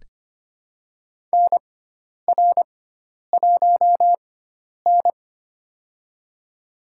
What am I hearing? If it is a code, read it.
Morse code, heard as a keyed tone, about 25 wpm, spaced as for 10 wpm, 700 Hz. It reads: NR1N
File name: Lesson 16g - Letter c - Call Signs 25wpm